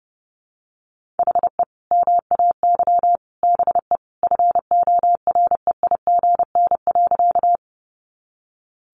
Morse code, heard as a keyed tone, about 30 wpm, 700 Hz. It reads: HE MAY BE FOREIGN.
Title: Sentences from Top 1400 Words 30wpm